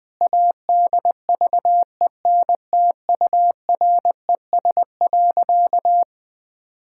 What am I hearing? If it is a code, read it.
Morse code, heard as a keyed tone, about 21 wpm, 700 Hz. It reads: ADVENTURES.
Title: Sentences with 7-Word Phrases - 20wpm